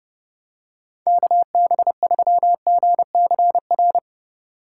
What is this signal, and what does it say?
Morse code, heard as a keyed tone, about 29 wpm, 700 Hz. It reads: KB3GCR